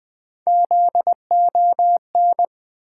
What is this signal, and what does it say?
Morse code, heard as a keyed tone, about 20 wpm, 700 Hz. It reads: ZON
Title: Build A Word - 1 Letter At A Time - 4-Letter Words - 20wpm